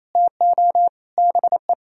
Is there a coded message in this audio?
Yes: Morse code from 0.1 to 1.7 s, heard as a keyed tone, about 37 wpm, 700 Hz. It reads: TO BE